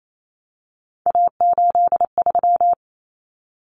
Morse code, heard as a keyed tone, about 28 wpm, 700 Hz. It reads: A83
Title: Common Cut Numbers - 3-Digit 28wpm